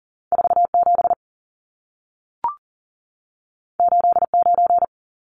Morse code, heard as a keyed tone, about 40 wpm, 700 Hz. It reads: 47 89
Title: Numbers - 2-Digit 40wpm